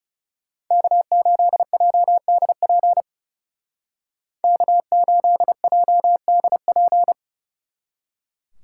Morse code, heard as a keyed tone, about 35 wpm, 700 Hz. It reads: K8JDP K8JDP
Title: Speed Racing - USA Call Signs 30wpm